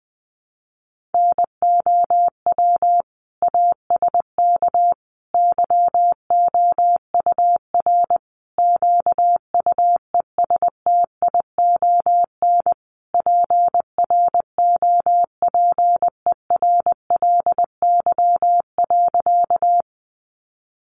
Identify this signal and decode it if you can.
Morse code, heard as a keyed tone, about 20 wpm, 700 Hz. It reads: NOW ASK YOUR QUESTION PROPERLY.
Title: Sentences from Top 1100 Words 20wpm